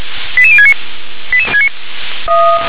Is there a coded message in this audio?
Yes: five-tone selective call (ZVEI) from 0.4 to 0.7 s, heard as quick runs of five tones, tones of about 70 ms. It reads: 88069